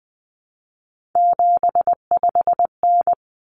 Morse code, heard as a keyed tone, about 20 wpm, 700 Hz. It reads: 75N
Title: Common Cut Numbers - 3-Digit 20wpm